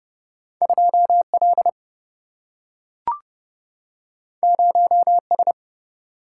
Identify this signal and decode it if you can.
Morse code, heard as a keyed tone, about 30 wpm, 700 Hz. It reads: 2 L 0 S
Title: Single-Letter Single-Number 30wpm